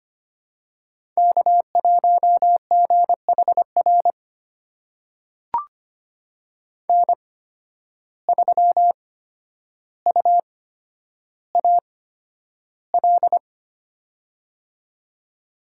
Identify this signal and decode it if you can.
Morse code, heard as a keyed tone, about 25 wpm, spaced as for 11 wpm, 700 Hz. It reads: K1GHR N3UAL
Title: USA Call Signs - 5 Characters - 8x Character Spacing - 25wpm